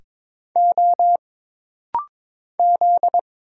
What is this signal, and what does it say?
Morse code, heard as a keyed tone, about 22 wpm, 700 Hz. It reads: O Z
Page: This rapid-fire timing is geared to push your ICR (Instant Character Recognition) skills.